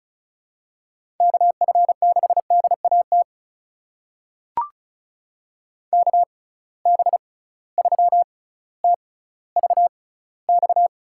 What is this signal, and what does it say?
Morse code, heard as a keyed tone, about 36 wpm, spaced as for 18 wpm, 700 Hz. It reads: KF6DAT KB3TVX
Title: USA Call Signs - 6 Characters - 6x Character Spacing - 35wpm